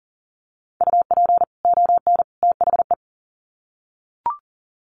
Morse code, heard as a keyed tone, about 40 wpm, 700 Hz. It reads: UP ON THE
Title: Sets of 3 Words 40wpm